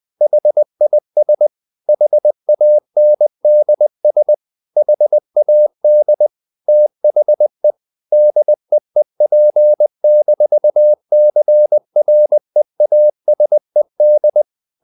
Morse code, heard as a keyed tone, about 20 wpm, 600 Hz. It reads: HIS HANDS HAD THE DEEP-CREASED
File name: the old man and the sea / the old man and the sea part 1 20wpm